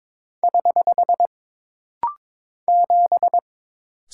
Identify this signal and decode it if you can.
Morse code, heard as a keyed tone, about 22 wpm, 700 Hz. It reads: <HH> 7
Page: This rapid-fire timing is geared to push your ICR (Instant Character Recognition) skills.